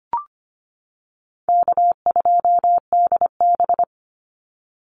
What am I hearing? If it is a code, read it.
Morse code, heard as a keyed tone, about 25 wpm, 700 Hz. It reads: K2DB
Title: Easy Call Signs - 25wpm